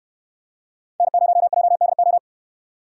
Morse code, heard as a keyed tone, about 65 wpm, 700 Hz. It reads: N0JDO